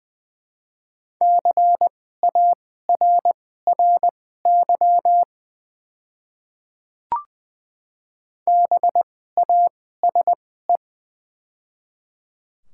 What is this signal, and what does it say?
Morse code, heard as a keyed tone, about 20 wpm, spaced as for 14 wpm, 690 Hz. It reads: CARRY BASE
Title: 2x Character Spacing - Top 300 Words - 20wpm